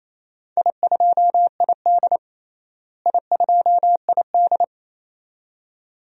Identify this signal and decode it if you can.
Morse code, heard as a keyed tone, about 28 wpm, 700 Hz. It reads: I2ID I2ID
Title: DXCC-100 Call Signs - 4 Characters - Repeated 2x - 28wpm